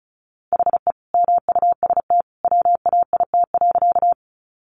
Morse code, heard as a keyed tone, about 34 wpm, 700 Hz. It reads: HE MUST WAIT.
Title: Sentences from Top 500 Words 35wpm